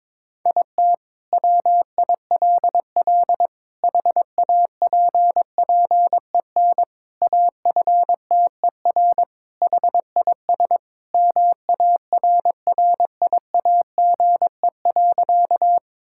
Morse code, heard as a keyed tone, about 22 wpm, 700 Hz. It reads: IT WILL HAPPEN AFTER HIS MARRIAGE.
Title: Sentences from Top 1000 Words - Review Part 5 - 22wpm